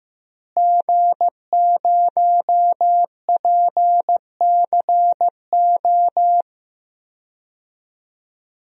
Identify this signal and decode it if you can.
Morse code, heard as a keyed tone, about 15 wpm, 700 Hz. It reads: G0PCO
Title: DXCC-100 Call Signs - 5 Characters - Repeated 3x - 15wpm